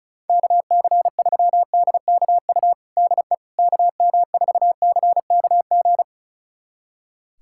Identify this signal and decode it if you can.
Morse code, heard as a keyed tone, about 34 wpm, 700 Hz. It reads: KC3DKU DE KM4CKG
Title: USA Callsign DE Callsign 35wpm